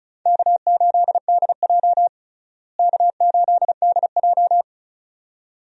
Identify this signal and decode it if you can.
Morse code, heard as a keyed tone, about 36 wpm, 700 Hz. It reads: K8DJ K8DJ
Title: USA Call Signs - 4 Characters - Repeated 2x - 35wpm